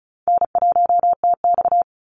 Morse code, heard as a keyed tone, about 34 wpm, 700 Hz. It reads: N1TX
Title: Easy Call Signs Repeated 3x - 35wpm